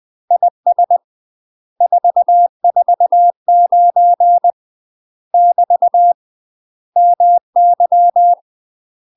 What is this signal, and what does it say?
Morse code, heard as a keyed tone, about 20 wpm, 700 Hz. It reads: IS 449 <BT> MY
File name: UK MASTER QSOs TRAINING / 20WPM / UK MASTER QSOs 0062